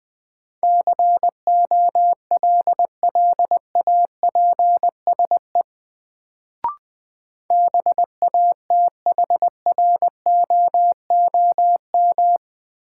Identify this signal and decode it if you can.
Morse code, heard as a keyed tone, about 20 wpm, 700 Hz. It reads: COLLAPSE BATHROOM
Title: Sets of 5 Words Encore - New Word Drill - 20wpm